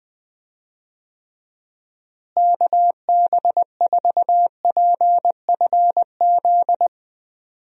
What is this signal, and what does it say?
Morse code, heard as a keyed tone, about 20 wpm, 700 Hz. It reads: KB4PFZ
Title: USA Call Signs - 6 Characters - Repeated 2x - 20wpm